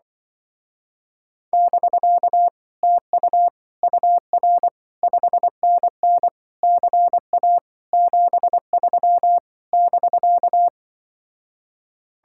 Morse code, heard as a keyed tone, about 25 wpm, 700 Hz. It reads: <BK> TU UR 5NN CA 73 <BK>